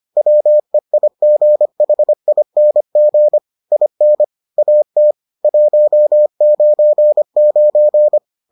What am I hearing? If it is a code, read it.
Morse code, heard as a keyed tone, about 25 wpm, 600 Hz. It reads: WEIGHING IN AT 199